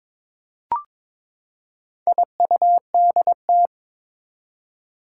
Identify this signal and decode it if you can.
Morse code, heard as a keyed tone, about 22 wpm, 700 Hz. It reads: IUDT
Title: Four Challenging Letter-Number Combinations 22wpm